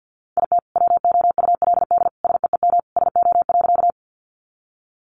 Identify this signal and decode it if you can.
Morse code, heard as a keyed tone, about 50 wpm, 700 Hz. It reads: IT WOULD SEEM SO.
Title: Sentences from Top 300 Words 50wpm